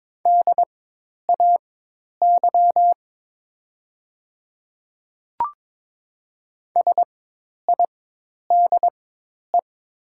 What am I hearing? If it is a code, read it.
Morse code, heard as a keyed tone, about 22 wpm, spaced as for 10 wpm, 700 Hz. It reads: DAY SIDE
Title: 4x Character Spacing - Top 500 Words - 22wpm